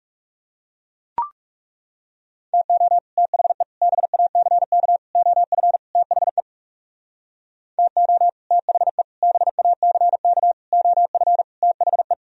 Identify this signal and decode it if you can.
Morse code, heard as a keyed tone, about 39 wpm, 700 Hz. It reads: TO THE BACK OF THE TO THE BACK OF THE